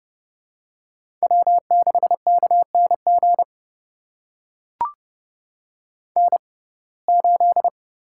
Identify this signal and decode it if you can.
Morse code, heard as a keyed tone, about 30 wpm, spaced as for 22 wpm, 700 Hz. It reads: W6KNG N8
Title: USA Call Signs - 5 Characters - 6x Character Spacing - 30wpm